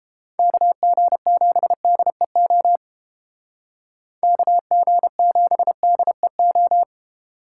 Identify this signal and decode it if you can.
Morse code, heard as a keyed tone, about 31 wpm, 700 Hz. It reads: KG7DEO KG7DEO